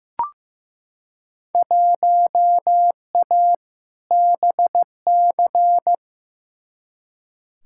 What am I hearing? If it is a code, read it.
Morse code, heard as a keyed tone, about 15 wpm, 700 Hz. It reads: 1A BC
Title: Field Day Exchanges - 15wpm